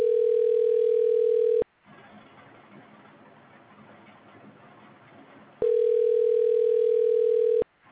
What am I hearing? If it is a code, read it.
Ringback tone (North America), 440 + 480 Hz.